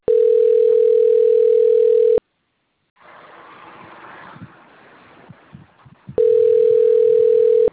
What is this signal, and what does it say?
Ringback tone (North America), 440 + 480 Hz.